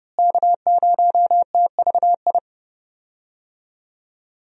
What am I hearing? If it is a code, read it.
Morse code, heard as a keyed tone, about 30 wpm, 700 Hz. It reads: K0TVI
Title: USA Call Signs - 5 Characters - Repeated 3x - 30wpm